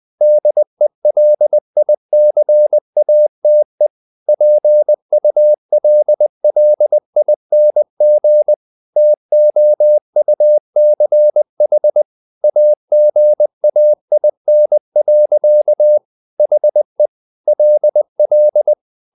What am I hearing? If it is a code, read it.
Morse code, heard as a keyed tone, about 20 wpm, 600 Hz. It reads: DELICATE PULLING TOUCH AGAIN. HE LL
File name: the old man and the sea / the old man and the sea part 1 20wpm